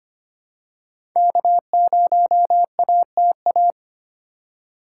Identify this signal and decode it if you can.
Morse code, heard as a keyed tone, about 25 wpm, 700 Hz. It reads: K0ATA
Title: USA Call Signs - 5 Characters - 2x Character Spacing - 25wpm